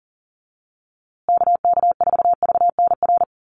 Morse code, heard as a keyed tone, about 40 wpm, 700 Hz. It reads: KK4VNR
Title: USA Call Signs Repeated 2x 40wpm